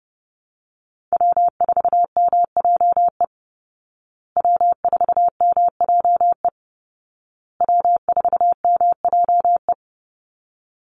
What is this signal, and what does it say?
Morse code, heard as a keyed tone, about 30 wpm, 700 Hz. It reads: W4MJE W4MJE W4MJE
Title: Call Signs Repeated 2x - 30wpm